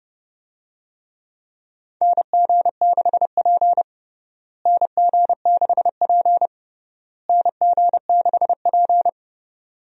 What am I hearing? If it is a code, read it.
Morse code, heard as a keyed tone, about 32 wpm, 700 Hz. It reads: NG6P NG6P NG6P
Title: USA Call Signs - 4 Characters - Repeated 3x - 30wpm